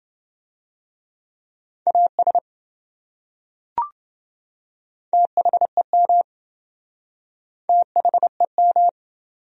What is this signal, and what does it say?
Morse code, heard as a keyed tone, about 27 wpm, 700 Hz. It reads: AS THEM THEM